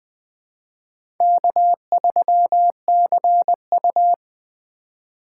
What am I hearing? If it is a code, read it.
Morse code, heard as a keyed tone, about 20 wpm, 700 Hz. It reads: K3CU
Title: 2023 POTA Hunter Call Signs - 20wpm